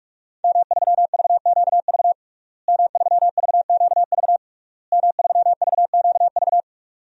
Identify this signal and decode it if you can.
Morse code, heard as a keyed tone, about 46 wpm, 700 Hz. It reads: M3VQV M3VQV M3VQV
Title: DXCC-100 Call Signs - 5 Characters - Repeated 3x - 45wpm